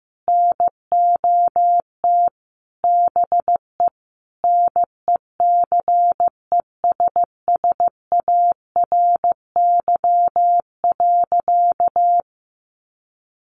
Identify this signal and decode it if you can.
Morse code, heard as a keyed tone, about 15 wpm, 700 Hz. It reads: NOT BE NECESSARY.
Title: Sentences from Top 800 Words 15wpm